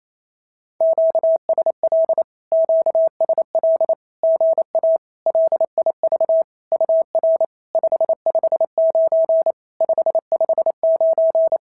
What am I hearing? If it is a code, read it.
Morse code, heard as a keyed tone, about 28 wpm, 650 Hz. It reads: QSL QSL GA LIV UR 559 559
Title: SOTA QSO Practice 28wpm